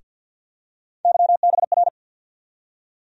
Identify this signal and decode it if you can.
Morse code, heard as a keyed tone, about 52 wpm, 690 Hz. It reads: YBR
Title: USA Call Sign Postfixes - 3 Letters 50wpm